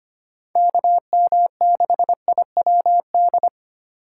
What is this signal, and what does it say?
Morse code, heard as a keyed tone, about 30 wpm, 700 Hz. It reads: KM6IWD
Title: Speed Racing - USA 2x3 Call Signs 25wpm